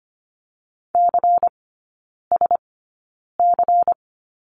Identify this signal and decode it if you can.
Morse code, heard as a keyed tone, about 25 wpm, spaced as for 9 wpm, 700 Hz. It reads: CSC
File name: course / 25:10wpm / 093 Lesson 16e - Letter c - Three Characters 25wpm